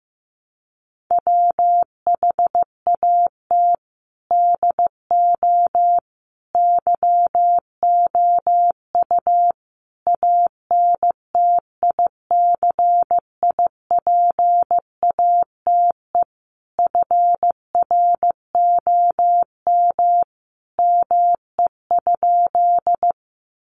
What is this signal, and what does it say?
Morse code, heard as a keyed tone, about 15 wpm, 700 Hz. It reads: WHAT DO YOU ANTICIPATE FROM ME?
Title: Sentences from Top 1500 Words 15wpm